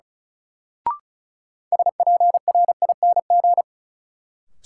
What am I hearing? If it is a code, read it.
Morse code, heard as a keyed tone, about 36 wpm, 700 Hz. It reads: SPRING
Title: Sentences from Top 1200 Words - New Word Drill Alt - 35wpm